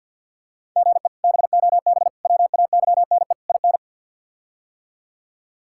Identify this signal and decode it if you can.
Morse code, heard as a keyed tone, about 51 wpm, 700 Hz. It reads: GE BOB WAYNE IN
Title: K1USN SST QSO Practice - 50wpm